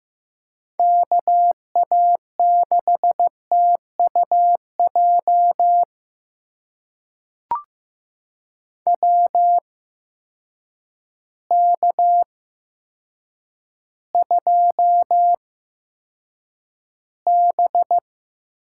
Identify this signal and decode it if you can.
Morse code, heard as a keyed tone, about 15 wpm, spaced as for 8 wpm, 700 Hz. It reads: KA6TUJ WK2B
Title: USA Call Signs - 8x Character Spacing - 15wpm